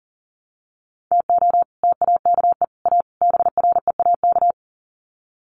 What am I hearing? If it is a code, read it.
Morse code, heard as a keyed tone, about 40 wpm, 700 Hz. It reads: TO TAKE A BREAK
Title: Sets of 4 Words 40wpm